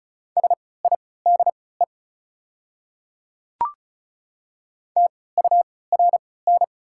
Morse code, heard as a keyed tone, about 36 wpm, spaced as for 20 wpm, 690 Hz. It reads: SIDE TURN
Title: 3x Character Spacing - Top 200 Words - 35wpm